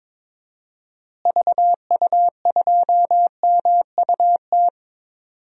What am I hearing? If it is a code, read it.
Morse code, heard as a keyed tone, about 22 wpm, 700 Hz. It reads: VU2MUT
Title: DXCC-100 Call Signs - 6 Characters - Repeated 2x - 22wpm